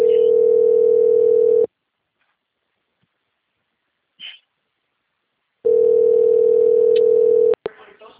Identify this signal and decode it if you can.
Ringback tone (North America), 440 + 480 Hz, with speech.
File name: out- - - - - .16708